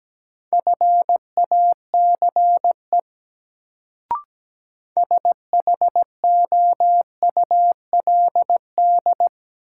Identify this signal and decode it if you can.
Morse code, heard as a keyed tone, about 17 wpm, 700 Hz. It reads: FACE SHOULD
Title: Sentences from Top 300 Words - New Word Drill - 17wpm